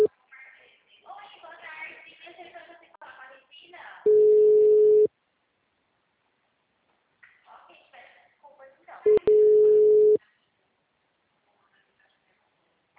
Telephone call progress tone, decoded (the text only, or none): ringback tone